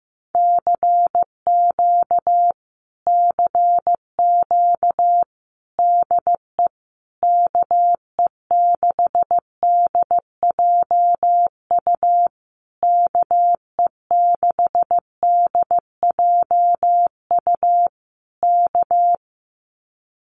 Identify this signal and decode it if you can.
Morse code, heard as a keyed tone, about 15 wpm, 700 Hz. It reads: CQ CQ DE KE6DJU KE6DJU K
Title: Common QSO Phrases Encore 15wpm